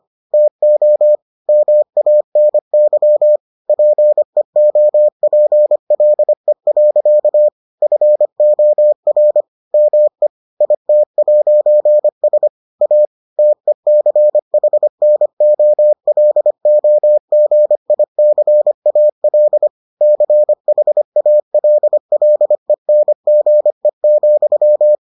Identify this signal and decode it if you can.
Morse code, heard as a keyed tone, about 25 wpm, 600 Hz. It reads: TO MANY PEOPLE. FOR ME IT'S A TECHNOLOGICAL CHALLENGE,